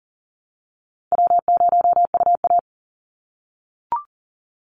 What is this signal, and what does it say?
Morse code, heard as a keyed tone, about 39 wpm, 700 Hz. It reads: W0UA
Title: Call Signs - 40wpm